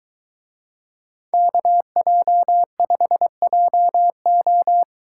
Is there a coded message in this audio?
Yes: Morse code from 1.3 to 4.8 s, heard as a keyed tone, about 23 wpm, 700 Hz. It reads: KJ5JO